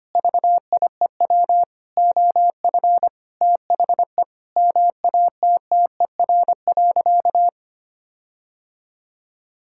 Morse code, heard as a keyed tone, about 25 wpm, 700 Hz. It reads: VIEW OF THE MATTER.
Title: Sentences from Top 1600 Words 25wpm